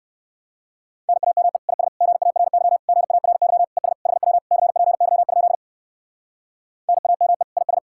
Morse code, heard as a keyed tone, about 59 wpm, 700 Hz. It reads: NAME IS GARY GARY ES HW CPY? NAME IS